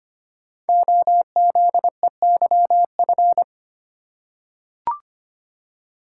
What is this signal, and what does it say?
Morse code, heard as a keyed tone, about 25 wpm, 700 Hz. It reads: OZEYF
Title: Code Groups - 5 Letter:Numbers - 25wpm